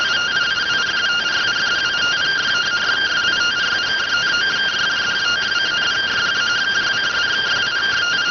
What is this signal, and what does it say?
Two-tone teleprinter signal, about 170 Hz shift.